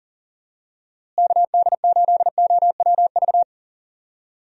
Morse code, heard as a keyed tone, about 40 wpm, 700 Hz. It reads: KD8OWV
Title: USA Call Signs - 6 Characters - Repeated 2x - 40wpm